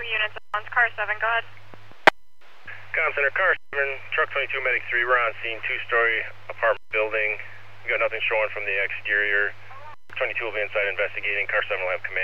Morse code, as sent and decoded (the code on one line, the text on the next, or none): none